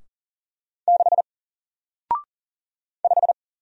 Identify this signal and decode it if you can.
Morse code, heard as a keyed tone, about 40 wpm, 700 Hz. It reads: 6 5
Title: Single Character (Letter:Number:Punctuation:Prosign) Alt 40wpm